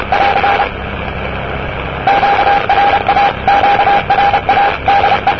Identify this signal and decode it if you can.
Morse code, heard as a keyed tone, about 27 wpm, 800 Hz, with radio noise. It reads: <AR> QRA QRA DE
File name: europe / e-539-cw-eav988-rb-1973